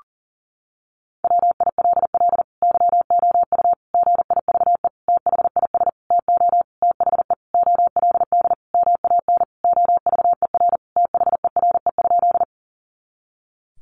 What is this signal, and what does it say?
Morse code, heard as a keyed tone, about 40 wpm, 700 Hz. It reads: WILL YOU GIVE THIS TO THE OLD MAN OVER THERE?
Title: Sentences from Top 200 Words 40wpm